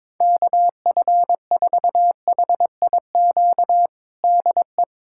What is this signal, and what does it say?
Morse code, heard as a keyed tone, about 22 wpm, 700 Hz. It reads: KF4HIQ DE
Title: USA Callsign DE Callsign 22wpm